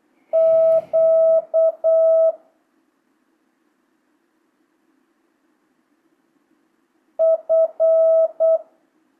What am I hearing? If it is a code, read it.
Morse code, heard as a keyed tone, about 8 wpm, 650 Hz. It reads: Q F
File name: jh8cbh / 2025cw